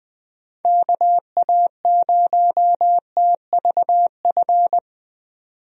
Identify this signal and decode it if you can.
Morse code, heard as a keyed tone, about 20 wpm, 700 Hz. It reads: KA0TVF
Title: USA Call Signs - 6 Characters - 20wpm